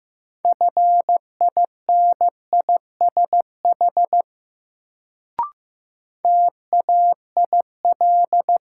Morse code, heard as a keyed tone, about 15 wpm, 700 Hz. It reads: FINISH TAIL
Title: Sentences from Top 1000 Words - Review Part 3 - New Word Drill - 15wpm